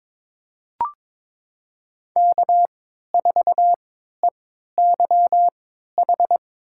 Morse code, heard as a keyed tone, about 22 wpm, spaced as for 12 wpm, 700 Hz. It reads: K4EYH
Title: USA Call Signs - 5 Characters - 3x Character Spacing - 22wpm